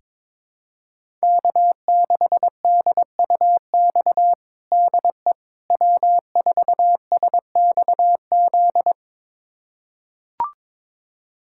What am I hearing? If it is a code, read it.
Morse code, heard as a keyed tone, about 21 wpm, 700 Hz. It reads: K6DUX DE W4SXZ
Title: USA Callsign DE Callsign - 5 Characters - 22wpm